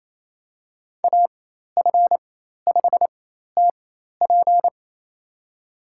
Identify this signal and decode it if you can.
Morse code, heard as a keyed tone, about 27 wpm, spaced as for 13 wpm, 700 Hz. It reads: AF5TP
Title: USA Call Signs - 5 Characters - 4x Character Spacing - 28wpm